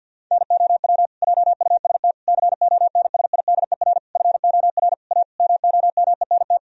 Morse code, heard as a keyed tone, about 50 wpm, 700 Hz. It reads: NOW JUST CONSIDER FOR A MOMENT
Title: Sentences from Top 1000 Words - Review Part 4 - 50wpm